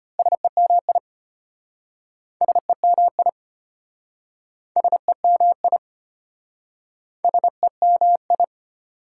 Morse code, heard as a keyed tone, about 33 wpm, 700 Hz. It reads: SEMI SEMI SEMI SEMI